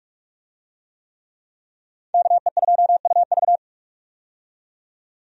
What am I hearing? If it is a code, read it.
Morse code, heard as a keyed tone, about 46 wpm, 700 Hz. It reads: KE2UV
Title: USA Call Signs - 5 Characters - 45wpm